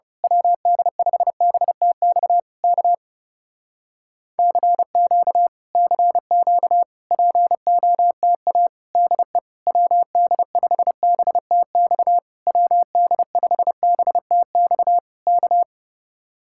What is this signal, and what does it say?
Morse code, heard as a keyed tone, about 28 wpm, 700 Hz. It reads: WD5BTX K CQ CQ POTA DE WD5BTX WD5BTX K
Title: Speed-Racing Common QSO Phrases Encore - 30wpm